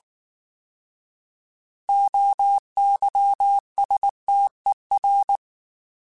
Morse code, heard as a keyed tone, about 19 wpm, 800 Hz. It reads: OYSTER